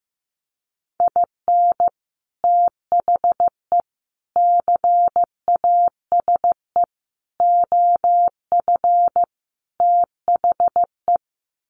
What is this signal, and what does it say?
Morse code, heard as a keyed tone, about 15 wpm, 700 Hz. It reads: IN THE CASE OF THE
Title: Sets of 5 Words 15wpm